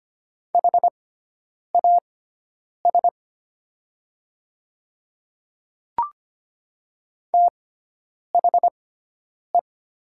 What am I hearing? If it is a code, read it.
Morse code, heard as a keyed tone, about 25 wpm, spaced as for 9 wpm, 700 Hz. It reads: HAS THE
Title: 6x Character Spacing - Top 100 Words - 25wpm